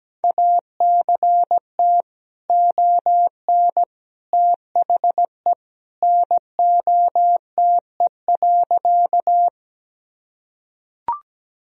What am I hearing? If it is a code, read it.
Morse code, heard as a keyed tone, about 17 wpm, 700 Hz. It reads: ACT ON THE NOTE.
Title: Sentences from Top 1000 Words - Review Part 2 - 17wpm